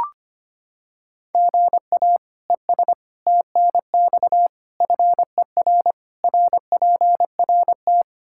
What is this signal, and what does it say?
Morse code, heard as a keyed tone, about 25 wpm, 700 Hz. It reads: GA ES TNX FER RPRT
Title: Practice QSO - Exchange 1 - Beginning - 25wpm